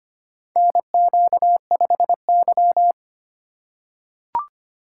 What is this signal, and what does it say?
Morse code, heard as a keyed tone, about 25 wpm, 700 Hz. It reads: NQ5Y
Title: USA Call Signs - 4 Characters - 4x Character Spacing - 25wpm